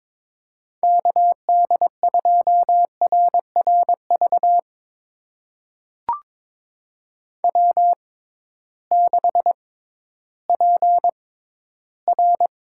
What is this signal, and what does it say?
Morse code, heard as a keyed tone, about 21 wpm, spaced as for 13 wpm, 700 Hz. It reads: KD2RRV W6PR
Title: USA Call Signs - 6x Character Spacing - 22wpm